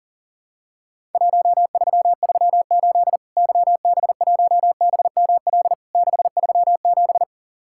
Morse code, heard as a keyed tone, about 40 wpm, 700 Hz. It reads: 1338 YB1BML 637
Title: Worked All Europe DX Contest - QTC Exchange - 40wpm